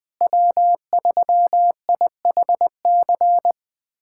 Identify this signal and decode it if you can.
Morse code, heard as a keyed tone, about 20 wpm, 700 Hz. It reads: W3IHC
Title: USA Callsign DE Callsign 20wpm